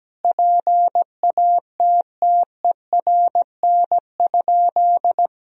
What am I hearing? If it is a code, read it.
Morse code, heard as a keyed tone, about 17 wpm, 700 Hz. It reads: PATTERN?
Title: Sentences from Top 1000 Words - Review Part 2 - 17wpm